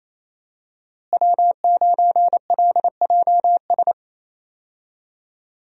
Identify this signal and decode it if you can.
Morse code, heard as a keyed tone, about 27 wpm, 700 Hz. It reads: W9LJS
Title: USA Call Signs - 5 Characters - Repeated 3x - 28wpm